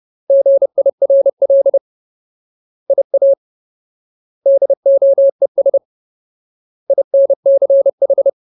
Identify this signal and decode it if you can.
Morse code, heard as a keyed tone, about 30 wpm, 550 Hz. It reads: GIRL IA DOES INCH
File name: Ultimate Word List / 30WPM Ultimate